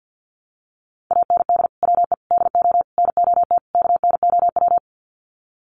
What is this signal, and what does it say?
Morse code, heard as a keyed tone, about 50 wpm, 700 Hz. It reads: AND WE DO NOT KNOW
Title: Sets of 5 Words Encore 50wpm